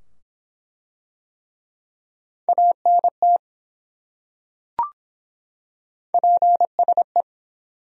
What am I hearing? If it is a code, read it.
Morse code, heard as a keyed tone, about 23 wpm, 700 Hz. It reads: ANT PSE